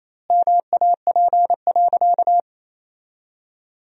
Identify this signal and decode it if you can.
Morse code, heard as a keyed tone, about 28 wpm, 700 Hz. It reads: MAP.
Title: 2x Word-Spacing - Sentences from Top 400 Words - 28wpm